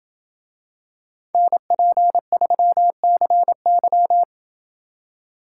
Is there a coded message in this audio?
Yes: Morse code from 1.3 to 4.2 s, heard as a keyed tone, about 27 wpm, 700 Hz. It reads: NP3CY